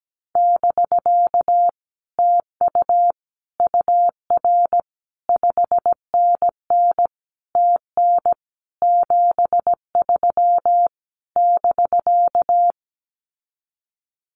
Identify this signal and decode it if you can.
Morse code, heard as a keyed tone, about 17 wpm, 700 Hz. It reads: <BK> TU UR 5NN TN 73 <BK>
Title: Common QSO Phrases Encore 17wpm